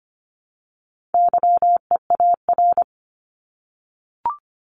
Morse code, heard as a keyed tone, about 25 wpm, 700 Hz. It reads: YEAR
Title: ES:S Suffix - Top 100 Words with Base Words 25wpm